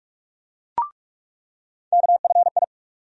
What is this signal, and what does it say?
Morse code, heard as a keyed tone, about 43 wpm, 700 Hz. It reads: KFI
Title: USA Call Sign Postfixes - 3 Letters 45wpm